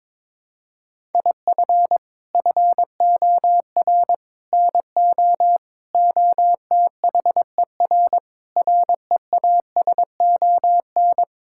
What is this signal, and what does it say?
Morse code, heard as a keyed tone, about 22 wpm, 700 Hz. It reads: IF FOR NO OTHER REASON
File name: practice / 22wpm / Sets of 7 Words - 22wpm